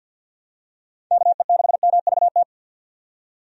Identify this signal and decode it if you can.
Morse code, heard as a keyed tone, about 53 wpm, 700 Hz. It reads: KE6MVT
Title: USA Call Signs - 6 Characters - 50wpm